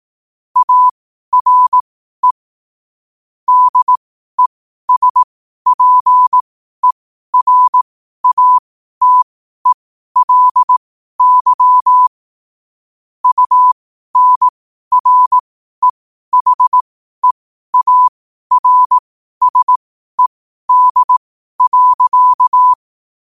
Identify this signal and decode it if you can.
Morse code, heard as a keyed tone, about 18 wpm, spaced as for 12 wpm, 1000 Hz. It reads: ARE DESPERATELY UNREHEARSED.
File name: qotdfiles / 12wpm